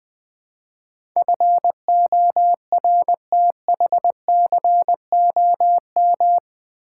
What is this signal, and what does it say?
Morse code, heard as a keyed tone, about 20 wpm, 700 Hz. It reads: FORTHCOM
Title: Build A Word - 4 Letters At A Time - 11-Letter Words - 20wpm